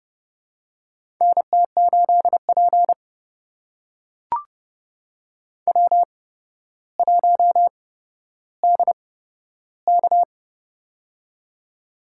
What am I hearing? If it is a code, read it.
Morse code, heard as a keyed tone, about 30 wpm, spaced as for 13 wpm, 700 Hz. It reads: NT8P W1DK
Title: USA Call Signs - 8x Character Spacing - 30wpm